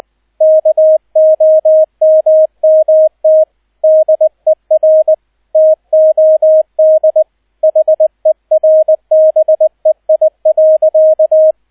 Morse code, heard as a keyed tone, about 20 wpm, 630 Hz, with radio noise. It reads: KOMMT DER TOD HERBEI.
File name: Erster Streich 20wpm